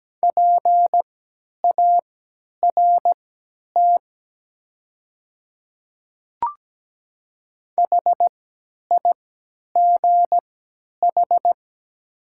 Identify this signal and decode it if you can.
Morse code, heard as a keyed tone, about 17 wpm, spaced as for 10 wpm, 700 Hz. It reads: PART HIGH
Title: 3x Character Spacing - Top 300 Words - 17wpm